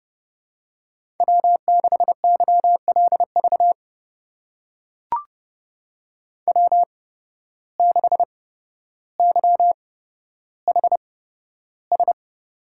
Morse code, heard as a keyed tone, about 30 wpm, spaced as for 13 wpm, 700 Hz. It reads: W6YLV W6YHS
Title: USA Call Signs - 5 Characters - 8x Character Spacing - 30wpm